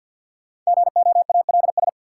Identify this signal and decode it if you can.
Morse code, heard as a keyed tone, about 53 wpm, 700 Hz. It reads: GOALS
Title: Sentences from Top 1000 Words - Review Part 6 - New Word Drill - 50wpm